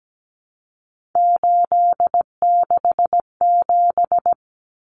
Morse code, heard as a keyed tone, about 17 wpm, 700 Hz. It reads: 867
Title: Common Cut Numbers - 3-Digit 17wpm